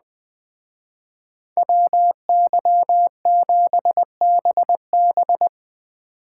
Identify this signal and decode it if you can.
Morse code, heard as a keyed tone, about 20 wpm, 700 Hz. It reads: WY7BB